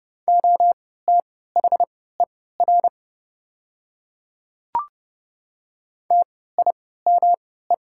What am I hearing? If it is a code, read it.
Morse code, heard as a keyed tone, about 30 wpm, spaced as for 17 wpm, 690 Hz. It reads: OTHER TIME
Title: 3x Character Spacing - Top 200 Words - 30wpm